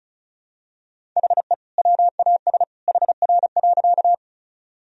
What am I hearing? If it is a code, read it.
Morse code, heard as a keyed tone, about 34 wpm, 700 Hz. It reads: HE WAS HR.
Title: Sentences from Top 200 Words and Abbreviations 35wpm